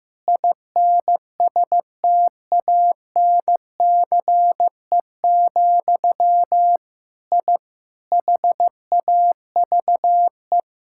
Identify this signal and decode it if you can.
Morse code, heard as a keyed tone, about 15 wpm, 700 Hz. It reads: INSTANCE, I HAVE
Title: Sentences from Top 1000 Words - Review Part 5 - 15wpm